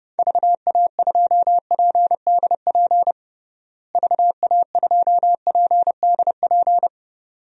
Morse code, heard as a keyed tone, about 32 wpm, 700 Hz. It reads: VA2PDP VA2PDP
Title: DXCC-100 Call Signs - 6 Characters - Repeated 2x - 30wpm